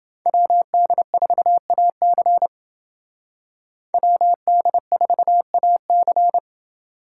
Morse code, heard as a keyed tone, about 29 wpm, 700 Hz. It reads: WD4AC WD4AC